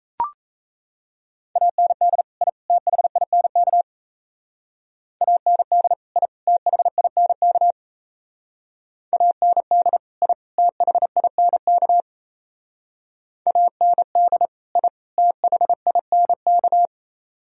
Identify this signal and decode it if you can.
Morse code, heard as a keyed tone, about 35 wpm, 700 Hz. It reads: AND I THINK AND I THINK AND I THINK AND I THINK